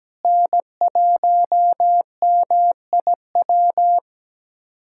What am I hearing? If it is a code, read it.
Morse code, heard as a keyed tone, about 17 wpm, 700 Hz. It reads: N1MIW
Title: USA Callsign DE Callsign - 5 Characters - 17wpm